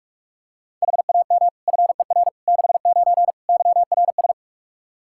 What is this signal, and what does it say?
Morse code, heard as a keyed tone, about 46 wpm, 700 Hz. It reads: HAM FER 69 YRS
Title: Practice QSO - Exchange 3 - More Info - 45wpm